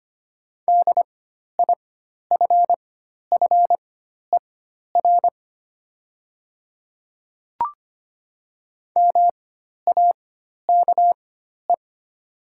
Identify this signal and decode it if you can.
Morse code, heard as a keyed tone, about 25 wpm, spaced as for 12 wpm, 700 Hz. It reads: DIFFER MAKE
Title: 4x Character Spacing - Top 200 Words - 25wpm